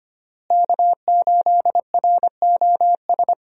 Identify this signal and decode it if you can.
Morse code, heard as a keyed tone, about 25 wpm, 700 Hz. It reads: K8ROS
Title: USA Call Signs - 5 Characters - Repeated 3x - 25wpm